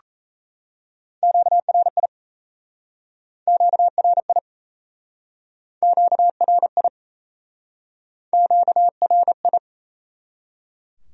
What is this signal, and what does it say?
Morse code, heard as a keyed tone, about 38 wpm, 700 Hz. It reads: QRI QRI QRI QRI